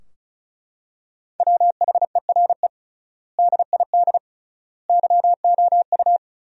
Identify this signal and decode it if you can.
Morse code, heard as a keyed tone, about 36 wpm, 700 Hz. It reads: WHERE DID YOU
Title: 3x Word-Spacing - Sentences from Top 200 Words - 35wpm